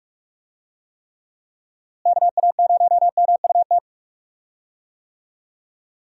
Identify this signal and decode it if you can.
Morse code, heard as a keyed tone, about 43 wpm, 700 Hz. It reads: KA0MUT
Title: USA Call Signs - 6 Characters - Repeated 3x - 45wpm